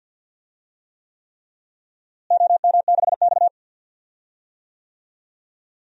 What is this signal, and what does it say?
Morse code, heard as a keyed tone, about 46 wpm, 700 Hz. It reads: OM6X
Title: DXCC-100 Call Signs - 4 Characters - Repeated 3x - 50wpm